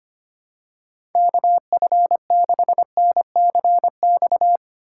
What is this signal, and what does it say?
Morse code, heard as a keyed tone, about 25 wpm, 700 Hz. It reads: KF6NCX
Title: K1USN SST QSO Practice - 25wpm